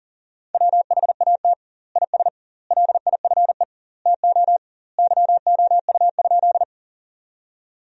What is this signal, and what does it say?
Morse code, heard as a keyed tone, about 40 wpm, 700 Hz. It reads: WHAT IS LIFE TO YOU?
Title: 2x Word-Spacing - Sentences from Top 300 Words - 40wpm